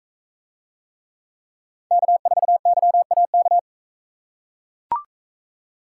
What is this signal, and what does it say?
Morse code, heard as a keyed tone, about 41 wpm, 700 Hz. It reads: K4YAK